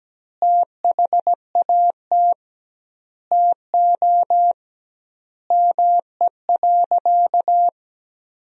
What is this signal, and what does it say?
Morse code, heard as a keyed tone, about 17 wpm, 700 Hz. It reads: THAT TO ME.
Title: 2x Word-Spacing - Sentences from Top 400 Words - 17wpm